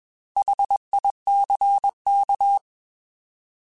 Morse code, heard as a keyed tone, about 21 wpm, 800 Hz. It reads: HICK
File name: Alphabetical / Alphabetical mp3s / Hs @20